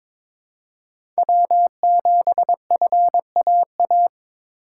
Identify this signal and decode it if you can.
Morse code, heard as a keyed tone, about 22 wpm, 700 Hz. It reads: W7FAA
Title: USA Call Signs - 4x Character Spacing - 22wpm